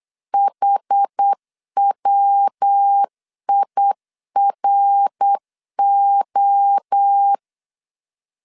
Morse code, heard as a keyed tone, about 8 wpm, 800 Hz. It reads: HWIRO
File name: CW / DET / 1 3 1500C S B 8wpm 4141